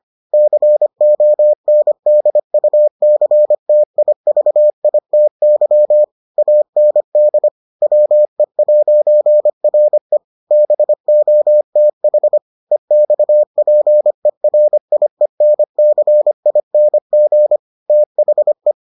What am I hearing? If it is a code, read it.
Morse code, heard as a keyed tone, about 25 wpm, 600 Hz. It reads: CONDUCTIVITY AND WE'RE BOTH EXPERIENCING THE